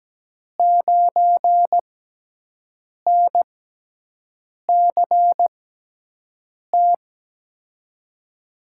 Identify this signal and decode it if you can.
Morse code, heard as a keyed tone, about 17 wpm, spaced as for 6 wpm, 700 Hz. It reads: 9NCT
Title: USA Call Signs - 6 Characters - 6x Character Spacing - 17wpm